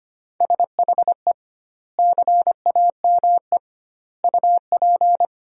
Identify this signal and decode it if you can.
Morse code, heard as a keyed tone, about 25 wpm, 700 Hz. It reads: SHE CAME UP
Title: 2x Word-Spacing - Sentences from Top 200 Words - 25wpm